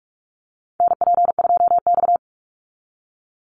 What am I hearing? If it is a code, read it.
Morse code, heard as a keyed tone, about 44 wpm, 700 Hz. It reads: NP2X
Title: Call Signs - 45wpm